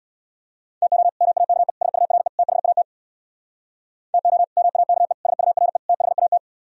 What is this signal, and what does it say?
Morse code, heard as a keyed tone, about 64 wpm, 700 Hz. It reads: TO MAKE SURE THAT TO MAKE SURE THAT